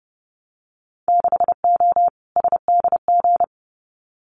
Morse code, heard as a keyed tone, about 30 wpm, 700 Hz. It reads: 6O SDG
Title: Winter Field Day Exchanges Encore 30wpm